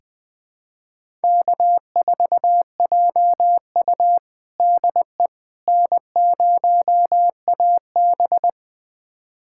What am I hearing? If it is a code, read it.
Morse code, heard as a keyed tone, about 20 wpm, 700 Hz. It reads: K4JU DE N0AB
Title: USA Callsign DE Callsign - 4 Characters - 20wpm